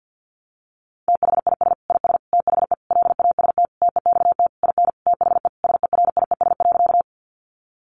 Morse code, heard as a keyed tone, about 50 wpm, 700 Hz. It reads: THIS IS THE LAST TEXT IN THE SERIES.
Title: Sentences from Top 1200 Words 50wpm